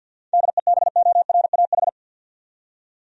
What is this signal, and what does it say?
Morse code, heard as a keyed tone, about 50 wpm, 700 Hz. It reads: DEBORAH
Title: USA Common Names - 50wpm